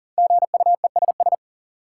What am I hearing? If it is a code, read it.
Morse code, heard as a keyed tone, about 38 wpm, 700 Hz. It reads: GUESS
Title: Sentences from Top 1500 Words - New Word Drill - 40wpm